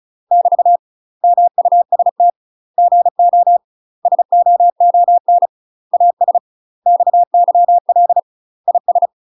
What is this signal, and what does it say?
Morse code, heard as a keyed tone, about 34 wpm, 700 Hz. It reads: <BT> MUST GO SOON AS XYL IS
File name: UK MASTER QSOs TRAINING / 35WPM / UK MASTER QSOs 0025